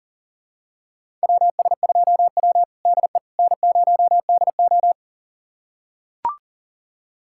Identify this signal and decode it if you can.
Morse code, heard as a keyed tone, about 40 wpm, 700 Hz. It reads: WS2W DE N0DO
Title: USA Callsign DE Callsign - 4 Characters - 40wpm